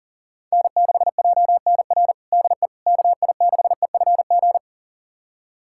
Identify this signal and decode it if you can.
Morse code, heard as a keyed tone, about 40 wpm, 700 Hz. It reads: N6JNR DE KI6EFG
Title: USA Callsign DE Callsign 40wpm